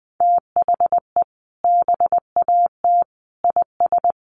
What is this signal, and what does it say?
Morse code, heard as a keyed tone, about 20 wpm, 700 Hz. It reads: THE BAT IS
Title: Sentences from Top 1500 Words 20wpm